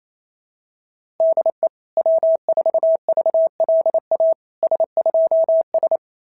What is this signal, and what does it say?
Morse code, heard as a keyed tone, about 27 wpm, 690 Hz. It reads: DE W4VLA S2S
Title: SOTA QSO Practice 28wpm